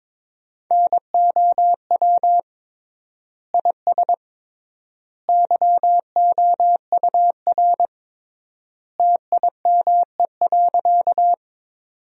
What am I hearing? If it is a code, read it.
Morse code, heard as a keyed tone, about 22 wpm, 700 Hz. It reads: NOW IS YOUR TIME.
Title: 3x Word-Spacing - Sentences from Top 100 Words - 22wpm